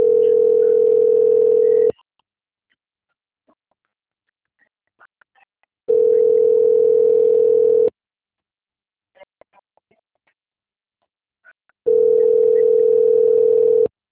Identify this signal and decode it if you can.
Ringback tone (North America), 440 + 480 Hz.